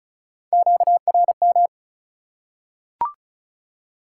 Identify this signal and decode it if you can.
Morse code, heard as a keyed tone, about 36 wpm, 700 Hz. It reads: QRM
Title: Top 180 QSO Elements 35wpm